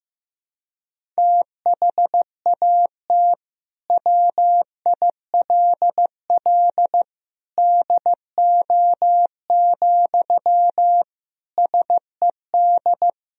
Morse code, heard as a keyed tone, about 15 wpm, 700 Hz. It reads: THAT WILL DO, SED
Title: Sentences from Top 100 Words and Abbreviations 15wpm